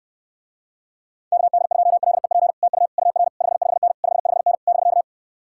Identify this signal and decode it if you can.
Morse code, heard as a keyed tone, about 69 wpm, 700 Hz. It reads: KD2LEP TU UR 55N 55N <BK>